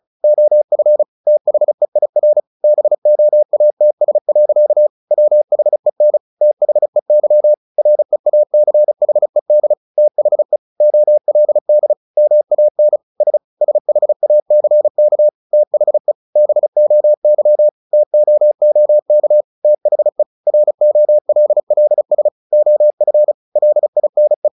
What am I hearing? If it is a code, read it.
Morse code, heard as a keyed tone, about 34 wpm, 600 Hz. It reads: OF THEIR BOATS. WHEN THEY REACHED THE OLD MAN S SHACK THE BOY TOOK THE ROLLS OF LINE
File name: the old man and the sea / the old man and the sea 1 35wpm